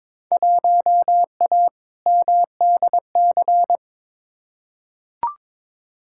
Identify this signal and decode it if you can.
Morse code, heard as a keyed tone, about 22 wpm, 700 Hz. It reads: 1A MDC
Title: Field Day Exchanges - 22wpm